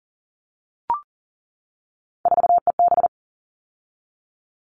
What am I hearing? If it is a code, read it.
Morse code, heard as a keyed tone, about 40 wpm, 700 Hz. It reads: 4EB
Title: All Cut Numbers - 3-Digit 40wpm